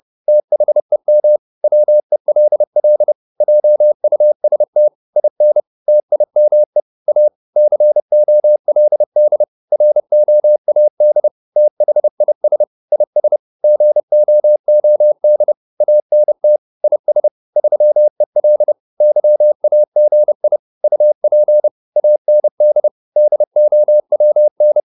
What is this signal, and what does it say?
Morse code, heard as a keyed tone, about 30 wpm, 600 Hz. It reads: THEM WELL JUST IN TIME A COLD ROAD THIS IS GOOD ANT IS 3EL YAGI UP AND DOWN